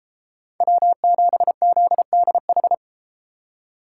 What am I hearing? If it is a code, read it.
Morse code, heard as a keyed tone, about 32 wpm, 700 Hz. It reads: W7ZDH